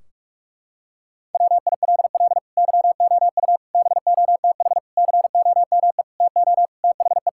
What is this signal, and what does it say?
Morse code, heard as a keyed tone, about 42 wpm, 700 Hz. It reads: WILL YOU BOTH COME TO THE
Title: Sentences from Top 1600 Words 45wpm